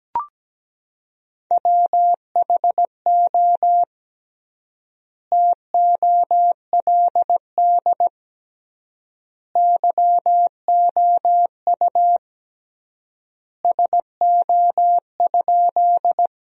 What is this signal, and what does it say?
Morse code, heard as a keyed tone, about 17 wpm, 700 Hz. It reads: WHO TOLD YOU SO?
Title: 3x Word-Spacing - Sentences from Top 400 Words - 17wpm